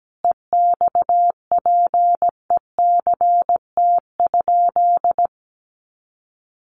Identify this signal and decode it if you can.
Morse code, heard as a keyed tone, about 17 wpm, 700 Hz. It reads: EXPECT?
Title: Sentences from Top 1100 Words 17wpm